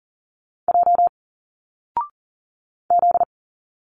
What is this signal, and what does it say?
Morse code, heard as a keyed tone, about 40 wpm, 700 Hz. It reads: J Z
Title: Single Letters 40wpm